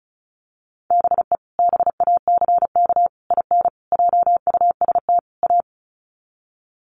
Morse code, heard as a keyed tone, about 34 wpm, 700 Hz. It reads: BE BACK IN JUST A
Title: Sets of 5 Words Encore 35wpm